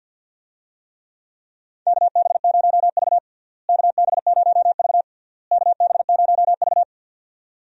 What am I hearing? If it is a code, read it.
Morse code, heard as a keyed tone, about 51 wpm, 700 Hz. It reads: KB0V KB0V KB0V
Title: USA Call Signs - 4 Characters - Repeated 3x - 50wpm